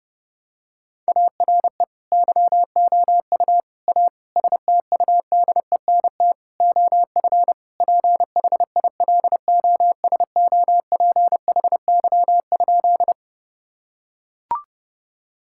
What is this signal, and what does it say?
Morse code, heard as a keyed tone, about 30 wpm, 700 Hz. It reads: ARE YOU A STUDENT OF PHILOSOPHY?
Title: Sentences from Top 1000 Words - Review Part 6 - 30wpm